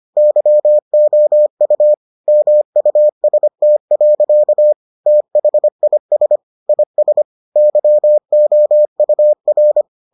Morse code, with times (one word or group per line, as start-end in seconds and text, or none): 0.2-1.9 YOU
2.3-4.7 MUST.
5.1-6.4 THIS
6.7-7.2 IS
7.6-9.8 YOUR